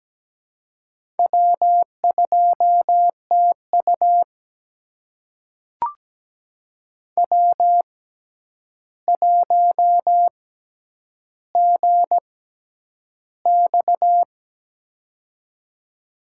Morse code, heard as a keyed tone, about 17 wpm, spaced as for 9 wpm, 700 Hz. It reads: W2TU W1GX
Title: USA Call Signs - 4 Characters - 6x Character Spacing - 17wpm